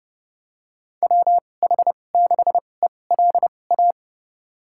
Morse code, heard as a keyed tone, about 30 wpm, spaced as for 22 wpm, 700 Hz. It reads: WH6ELA
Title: USA Call Signs - 6 Characters - 2x Character Spacing - 30wpm